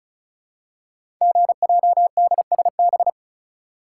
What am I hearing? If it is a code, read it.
Morse code, heard as a keyed tone, about 36 wpm, 700 Hz. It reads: GJDSB
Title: Code Groups - 5 Letters - 35wpm